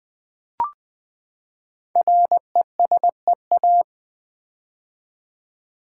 Morse code, heard as a keyed tone, about 20 wpm, 700 Hz. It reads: RESEA
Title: Build A Word - 5 Letters At A Time - 10-Letter Words - 20wpm